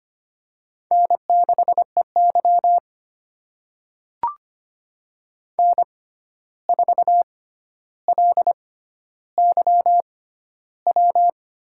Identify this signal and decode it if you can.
Morse code, heard as a keyed tone, about 25 wpm, spaced as for 12 wpm, 700 Hz. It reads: N6EY N4LYW
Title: USA Call Signs - 6x Character Spacing - 25wpm